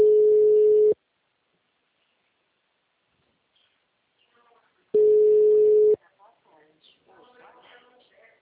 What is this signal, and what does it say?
Ringback tone (Europe, 425 Hz), with speech.
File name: out- -2004- - - .63275